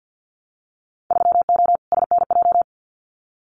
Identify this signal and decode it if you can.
Morse code, heard as a keyed tone, about 50 wpm, 700 Hz. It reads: 3O SNJ
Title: Winter Field Day Exchanges Encore 50wpm